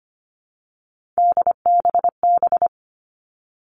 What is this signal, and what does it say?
Morse code, heard as a keyed tone, about 25 wpm, 700 Hz. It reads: DBB
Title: Three Challenging Letter-Number Combinations 25wpm